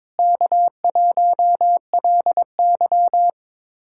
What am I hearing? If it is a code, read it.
Morse code, heard as a keyed tone, about 22 wpm, 700 Hz. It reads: K1LY
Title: USA Callsign DE Callsign - 4 Characters - 22wpm